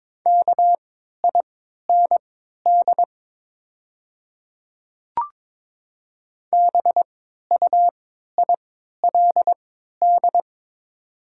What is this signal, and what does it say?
Morse code, heard as a keyed tone, about 22 wpm, spaced as for 12 wpm, 700 Hz. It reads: KIND BUILD
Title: 3x Character Spacing - Top 200 Words - 22wpm